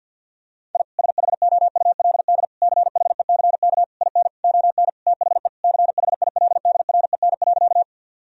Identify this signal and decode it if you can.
Morse code, heard as a keyed tone, about 47 wpm, 700 Hz. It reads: I SHOULD CHECK IN ON THE CHILDREN.
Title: Sentences from Top 1000 Words - Review Part 3 - 50wpm